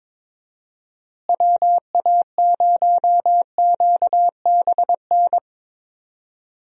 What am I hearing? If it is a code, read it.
Morse code, heard as a keyed tone, about 22 wpm, 700 Hz. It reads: WA0QBN
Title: USA Call Signs - 6 Characters - Repeated 3x - 22wpm